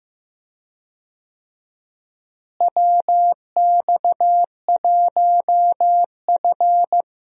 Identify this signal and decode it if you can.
Morse code, heard as a keyed tone, about 15 wpm, 700 Hz. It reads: WX1F
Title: USA Call Signs - 4 Characters - Repeated 3x - 15wpm